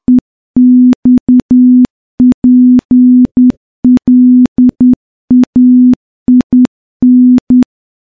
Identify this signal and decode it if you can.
Morse code, heard as a keyed tone, about 10 wpm, 250 Hz. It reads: EXPLAIN
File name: morse-training-10WPM